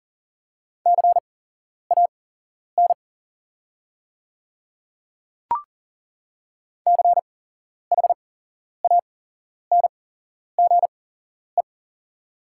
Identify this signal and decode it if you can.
Morse code, heard as a keyed tone, about 40 wpm, spaced as for 11 wpm, 700 Hz. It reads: CAN CHANGE
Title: 8x Character Spacing - Top 200 Words - 40wpm